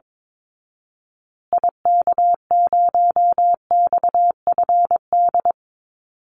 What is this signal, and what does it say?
Morse code, heard as a keyed tone, about 22 wpm, 700 Hz. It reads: IK0XFD
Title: Call Signs Repeated 3x - 22wpm